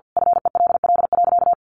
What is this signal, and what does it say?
Morse code, heard as a keyed tone, about 50 wpm, 700 Hz. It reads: FELL.
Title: A Sound Of Thunder 50wpm